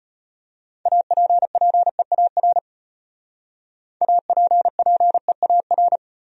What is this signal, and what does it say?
Morse code, heard as a keyed tone, about 31 wpm, 700 Hz. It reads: APPEAR APPEAR
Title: Speed-Racing Top 500 Words 25wpm